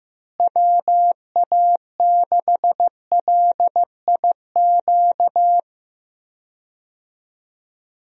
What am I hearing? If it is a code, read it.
Morse code, heard as a keyed tone, about 15 wpm, 700 Hz. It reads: WA6LIQ
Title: USA Call Signs - 6 Characters - Repeated 2x - 15wpm